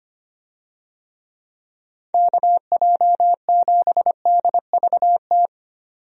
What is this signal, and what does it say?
Morse code, heard as a keyed tone, about 25 wpm, 700 Hz. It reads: KJ7DVT
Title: USA Call Signs - 6 Characters - Repeated 3x - 25wpm